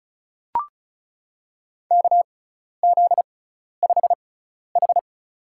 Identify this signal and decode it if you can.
Morse code, heard as a keyed tone, about 36 wpm, spaced as for 12 wpm, 700 Hz. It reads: KZ5H
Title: USA Call Signs - 6x Character Spacing - 35wpm